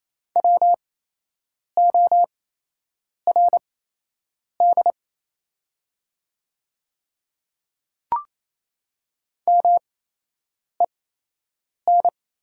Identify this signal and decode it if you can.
Morse code, heard as a keyed tone, about 28 wpm, spaced as for 8 wpm, 700 Hz. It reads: WORD MEN
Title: 8x Character Spacing - Top 300 Words - 28wpm